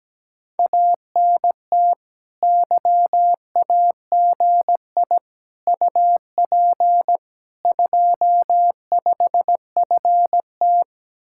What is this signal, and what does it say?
Morse code, heard as a keyed tone, about 17 wpm, 700 Hz. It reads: ANT YAGI UP 25FT
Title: Speed-Racing Common QSO Phrases - 17wpm